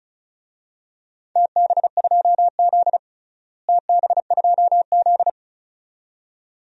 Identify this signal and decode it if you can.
Morse code, heard as a keyed tone, about 36 wpm, 690 Hz. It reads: TB2Z TB2Z
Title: DXCC-100 Call Signs - 4 Characters - Repeated 2x - 35wpm